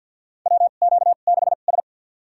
Morse code, heard as a keyed tone, about 46 wpm, spaced as for 35 wpm, 700 Hz. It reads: WQ6S
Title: USA Call Signs - 2x Character Spacing - 50wpm